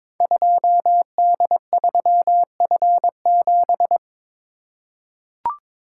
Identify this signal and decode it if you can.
Morse code, heard as a keyed tone, about 22 wpm, 690 Hz. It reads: 2D3F7
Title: Code Groups - 5 Letter:Numbers - 22wpm